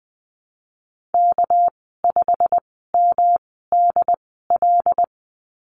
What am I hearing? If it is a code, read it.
Morse code, heard as a keyed tone, about 20 wpm, spaced as for 15 wpm, 700 Hz. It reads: K5MDL
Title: 207 Lesson 33g - Letter j - Call Signs 20wpm